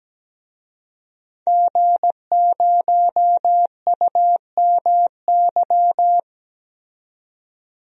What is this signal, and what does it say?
Morse code, heard as a keyed tone, about 17 wpm, 700 Hz. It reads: G0UMY
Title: DXCC-100 Call Signs - 5 Characters - 17wpm